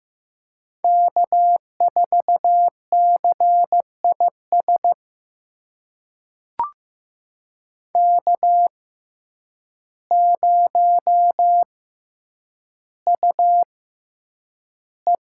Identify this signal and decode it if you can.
Morse code, heard as a keyed tone, about 15 wpm, spaced as for 8 wpm, 700 Hz. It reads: K4CIS K0UE
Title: USA Call Signs - 6x Character Spacing - 15wpm